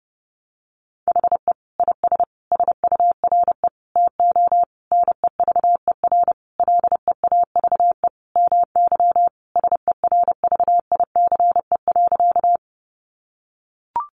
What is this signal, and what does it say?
Morse code, heard as a keyed tone, about 30 wpm, 700 Hz. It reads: HE IS SURE TO NEVER LEAVE MY SERVICE.
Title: Sentences from Top 700 Words 30wpm